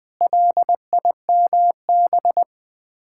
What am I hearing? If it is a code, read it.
Morse code, heard as a keyed tone, about 21 wpm, 700 Hz. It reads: LIMB
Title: Sets of 8 Words - 20wpm